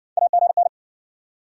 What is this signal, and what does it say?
Morse code, heard as a keyed tone, about 60 wpm, 700 Hz. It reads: AGN